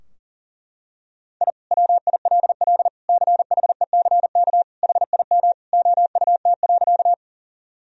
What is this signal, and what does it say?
Morse code, heard as a keyed tone, about 37 wpm, 700 Hz. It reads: I WILL CHECK HIM OUT.
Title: Sentences from Top 1000 Words - Review Part 3 - 40wpm